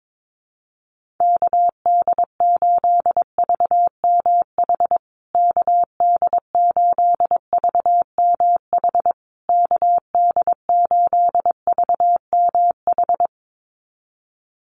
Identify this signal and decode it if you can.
Morse code, heard as a keyed tone, about 22 wpm, 700 Hz. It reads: KD8VMH KD8VMH KD8VMH
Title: USA Call Signs Repeated 3x 22wpm